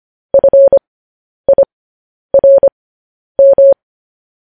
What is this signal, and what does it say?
Morse code, heard as a keyed tone, about 25 wpm, spaced as for 10 wpm, 550 Hz. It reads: FIRM
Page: news headlines in morse code at 10 wpm